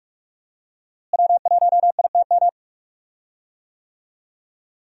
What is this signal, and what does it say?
Morse code, heard as a keyed tone, about 45 wpm, 700 Hz. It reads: W1ITM
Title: USA Call Signs - 5 Characters - Repeated 2x - 45wpm